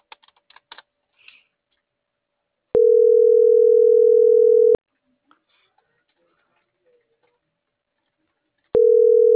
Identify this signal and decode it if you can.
Ringback tone (North America), 440 + 480 Hz.